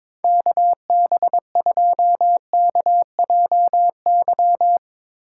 Morse code, heard as a keyed tone, about 22 wpm, 700 Hz. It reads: KB2KJY
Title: Speed Racing - USA 2x3 Call Signs 22wpm